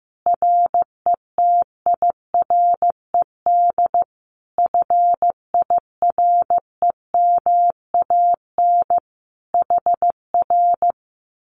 Morse code, heard as a keyed tone, about 15 wpm, 700 Hz. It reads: RETIRED FIREMAN HR
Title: Common QSO Phrases Encore 15wpm